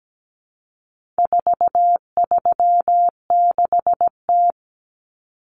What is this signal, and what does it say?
Morse code, heard as a keyed tone, about 17 wpm, 700 Hz. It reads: 436T
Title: Common Cut Numbers - 4-Digit 17wpm